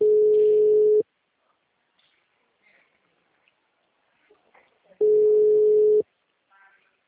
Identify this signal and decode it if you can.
Ringback tone (Europe, 425 Hz).
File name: out- -2007- - - .112